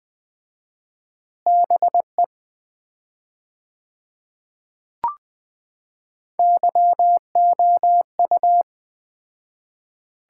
Morse code, heard as a keyed tone, about 20 wpm, 700 Hz. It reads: BE YOU
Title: 4x Word-Spacing - Sets of 5 Words - 20wpm